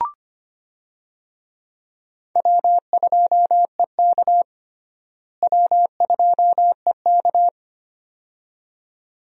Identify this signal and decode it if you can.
Morse code, heard as a keyed tone, about 25 wpm, 700 Hz. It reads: W2EK W2EK
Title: USA Call Signs - 4 Characters - Repeated 2x - 25wpm